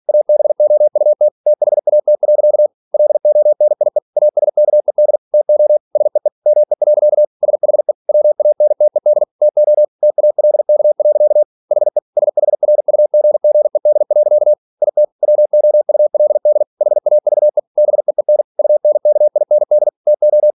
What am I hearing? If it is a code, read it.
Morse code, heard as a keyed tone, about 46 wpm, 600 Hz. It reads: ABOUT THAT. LONIE ASKED TO SEE ME. SHE WANTED TO TALK. HE SHRUGGED. IT WOULD HAVE BEEN UNKIND TO
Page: Never 47wpm